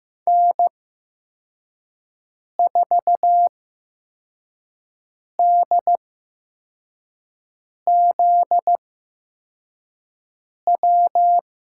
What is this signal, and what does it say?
Morse code, heard as a keyed tone, about 15 wpm, spaced as for 4 wpm, 700 Hz. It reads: N4DZW
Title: USA Call Signs - 5 Characters - 8x Character Spacing - 15wpm